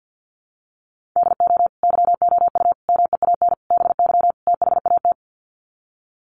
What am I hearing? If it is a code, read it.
Morse code, heard as a keyed tone, about 50 wpm, 700 Hz. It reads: DO YOU MEAN BY THAT
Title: Sets of 5 Words Encore2 50wpm